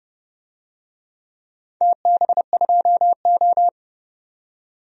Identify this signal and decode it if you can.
Morse code, heard as a keyed tone, about 30 wpm, 700 Hz. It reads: TB2O
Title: DXCC-100 Call Signs - 4 Characters - 30wpm